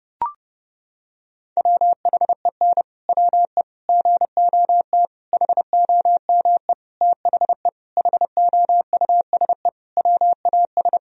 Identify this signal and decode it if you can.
Morse code, heard as a keyed tone, about 30 wpm, 700 Hz. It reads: WHEN WE GOT HOME THE HOUSE WAS
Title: Sentences from Top 500 Words and Abbreviations 30wpm